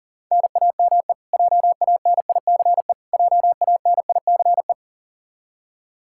Morse code, heard as a keyed tone, about 40 wpm, 700 Hz. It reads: NAME JANICE JANICE
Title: Practice QSO - Exchange 1 - Beginning - 40wpm